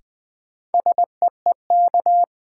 Morse code, heard as a keyed tone, about 20 wpm, 700 Hz. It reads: SEEK
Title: Sentences from Top 1600 Words - New Word Drill Alt - 20wpm